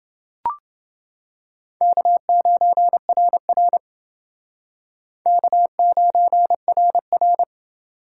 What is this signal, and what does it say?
Morse code, heard as a keyed tone, about 28 wpm, 700 Hz. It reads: K9RR K9RR